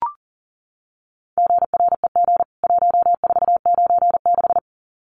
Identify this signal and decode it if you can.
Morse code, heard as a keyed tone, about 40 wpm, 700 Hz. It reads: GREG 1496
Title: CWT Exchange 40wpm